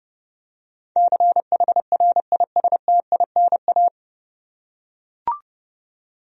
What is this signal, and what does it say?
Morse code, heard as a keyed tone, about 30 wpm, 700 Hz. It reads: CHRISTINA
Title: USA Common Names - 30wpm